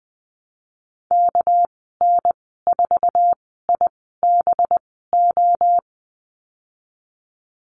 Morse code, heard as a keyed tone, about 19 wpm, spaced as for 14 wpm, 700 Hz. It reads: KN4IBO
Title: 207 Lesson 33g - Letter j - Call Signs 20wpm